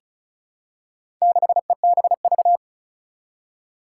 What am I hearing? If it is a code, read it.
Morse code, heard as a keyed tone, about 36 wpm, 690 Hz. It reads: 6EBV
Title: Four Challenging Letter-Number Combinations 35wpm